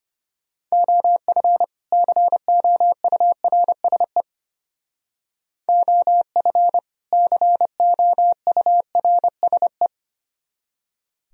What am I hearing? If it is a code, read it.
Morse code, heard as a keyed tone, about 33 wpm, 700 Hz. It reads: OF COURSE OF COURSE